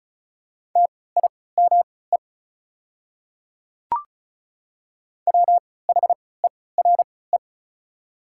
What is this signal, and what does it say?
Morse code, heard as a keyed tone, about 36 wpm, spaced as for 20 wpm, 700 Hz. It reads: TIME WHERE
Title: 3x Character Spacing - Top 200 Words - 35wpm